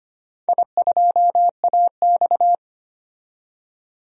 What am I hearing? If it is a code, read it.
Morse code, heard as a keyed tone, about 25 wpm, 700 Hz. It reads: I2AX
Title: DXCC-100 Call Signs - 4 Characters - 25wpm